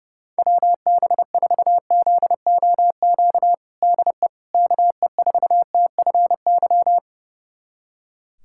Morse code, heard as a keyed tone, about 30 wpm, 700 Hz. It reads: WB4ZOQ DE KE4TFY
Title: USA Callsign DE Callsign 30wpm